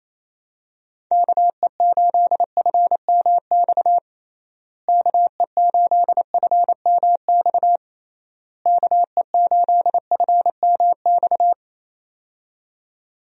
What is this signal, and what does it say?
Morse code, heard as a keyed tone, about 28 wpm, 700 Hz. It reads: KE8FMX KE8FMX KE8FMX
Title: USA Call Signs - 6 Characters - Repeated 3x - 28wpm